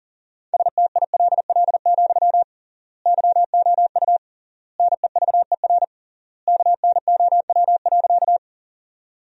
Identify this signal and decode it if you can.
Morse code, heard as a keyed tone, about 40 wpm, 700 Hz. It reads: STILL, YOU NEVER KNOW.
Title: 3x Word-Spacing - Sentences from Top 300 Words - 40wpm